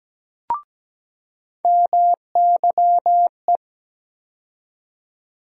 Morse code, heard as a keyed tone, about 17 wpm, 700 Hz. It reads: MYE
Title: USA Call Sign Postfixes - 3 Letters 17wpm